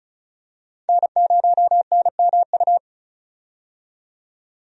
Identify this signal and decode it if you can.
Morse code, heard as a keyed tone, about 34 wpm, 700 Hz. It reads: N0NMU
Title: USA Call Signs - 5 Characters - 35wpm